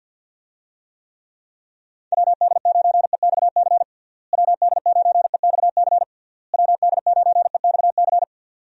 Morse code, heard as a keyed tone, about 52 wpm, 700 Hz. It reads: WD9EXC WD9EXC WD9EXC
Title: USA Call Signs - 6 Characters - Repeated 3x - 50wpm